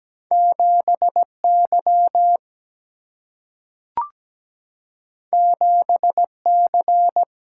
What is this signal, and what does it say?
Morse code, heard as a keyed tone, about 17 wpm, 700 Hz. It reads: 7Y 7C
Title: Single-Letter Single-Number 17wpm